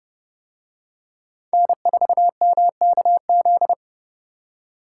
Morse code, heard as a keyed tone, about 31 wpm, 700 Hz. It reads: N4MKZ